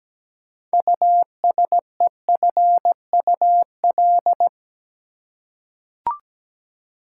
Morse code, heard as a keyed tone, about 17 wpm, 700 Hz. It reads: USEFUL
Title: Six Letter Words - 17wpm